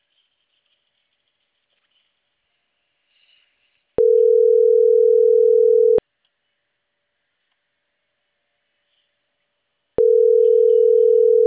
Ringback tone (North America), 440 + 480 Hz.
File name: CENTRAL CAEA / out agcaea01